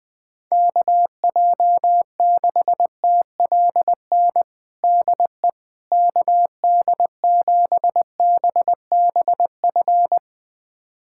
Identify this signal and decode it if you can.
Morse code, heard as a keyed tone, about 20 wpm, 700 Hz. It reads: KJ6TLN DE KD7BBF
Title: USA Callsign DE Callsign - 6 Characters - 20wpm